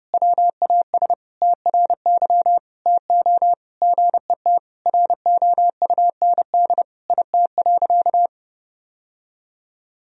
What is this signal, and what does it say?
Morse code, heard as a keyed tone, about 32 wpm, 700 Hz. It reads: WAS TRY TO GET ROUND IT.
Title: Sentences from Top 1000 Words - Review Part 1 - 30wpm